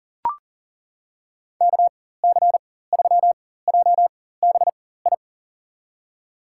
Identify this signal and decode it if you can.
Morse code, heard as a keyed tone, about 40 wpm, spaced as for 19 wpm, 700 Hz. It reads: KC3JBI
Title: USA Call Signs - 4x Character Spacing - 40wpm